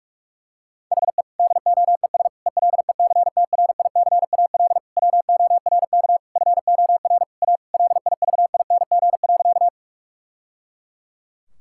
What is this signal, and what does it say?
Morse code, heard as a keyed tone, about 46 wpm, 700 Hz. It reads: HE DOES ELECTRICAL WORK FOR A LIVING.
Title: Sentences from Top 1000 Words - Review Part 6 - 45wpm